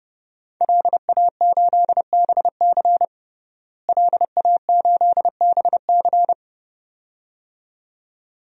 Morse code, heard as a keyed tone, about 31 wpm, 700 Hz. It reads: LA8BC LA8BC
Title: DXCC-100 Call Signs - 5 Characters - Repeated 2x - 30wpm